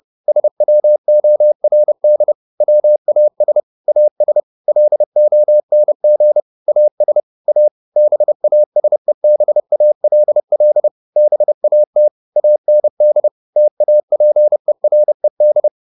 Morse code, heard as a keyed tone, about 30 wpm, 600 Hz. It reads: SWORD WAS AS LONG AS A BASEBALL BAT AND TAPERED
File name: the old man and the sea / the old man and the sea 1 30wpm